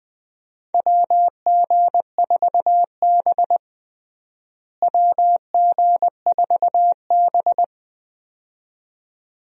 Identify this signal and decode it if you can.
Morse code, heard as a keyed tone, about 21 wpm, 700 Hz. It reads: WG4B WG4B
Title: USA Call Signs - 4 Characters - Repeated 2x - 20wpm